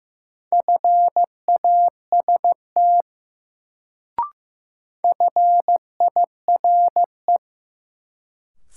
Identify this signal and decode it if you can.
Morse code, heard as a keyed tone, about 15 wpm, 700 Hz. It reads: FAST FIRE
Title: Sentences from Top 400 Words - New Word Drill - 15wpm